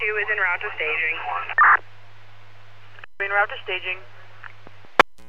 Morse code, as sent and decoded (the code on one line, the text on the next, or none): none